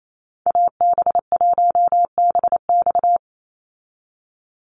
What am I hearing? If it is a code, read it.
Morse code, heard as a keyed tone, about 28 wpm, 700 Hz. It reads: AB1BX
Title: Call Signs Repeated 3x - 28wpm